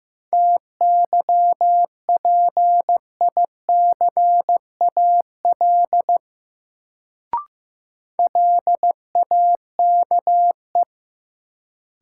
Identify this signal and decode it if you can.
Morse code, heard as a keyed tone, about 15 wpm, 700 Hz. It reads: TYPICAL LAKE
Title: Sentences from Top 1100 Words - New Word Drill Alt - 15wpm